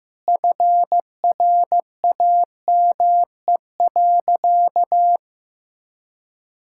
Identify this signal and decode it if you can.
Morse code, heard as a keyed tone, about 15 wpm, 700 Hz. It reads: FRAME.
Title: Sentences from Top 1000 Words - Review Part 6 - 15wpm